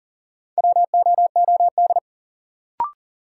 Morse code, heard as a keyed tone, about 40 wpm, 700 Hz. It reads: WOOD
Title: Sentences from Top 1000 Words - Review Part 2 - New Word Drill - 40wpm